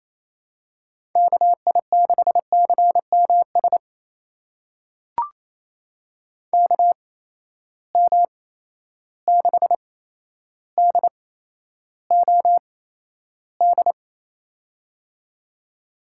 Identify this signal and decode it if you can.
Morse code, heard as a keyed tone, about 28 wpm, spaced as for 12 wpm, 700 Hz. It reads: KI6CMS KM6DOD
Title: USA Call Signs - 6 Characters - 8x Character Spacing - 28wpm